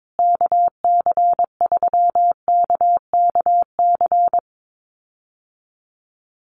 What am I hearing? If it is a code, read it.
Morse code, heard as a keyed tone, about 22 wpm, 700 Hz. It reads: KC3KKC
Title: USA Call Signs Repeated 3x 22wpm